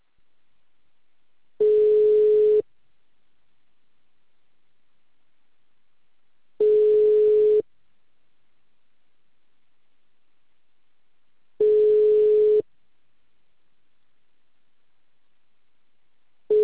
Ringback tone (Europe, 425 Hz).